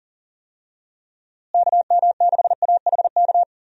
Morse code, heard as a keyed tone, about 37 wpm, 700 Hz. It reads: KM6AHK
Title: USA Call Signs - 6 Characters - Repeated 3x - 40wpm